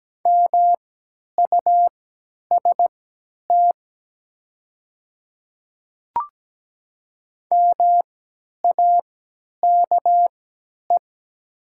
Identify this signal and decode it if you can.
Morse code, heard as a keyed tone, about 17 wpm, spaced as for 10 wpm, 700 Hz. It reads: MUST MAKE
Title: 3x Character Spacing - Top 200 Words - 17wpm